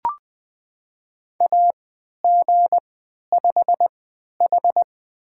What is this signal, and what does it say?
Morse code, heard as a keyed tone, about 20 wpm, spaced as for 11 wpm, 700 Hz. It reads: AG5H
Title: USA Call Signs - 4 Characters - 3x Character Spacing - 20wpm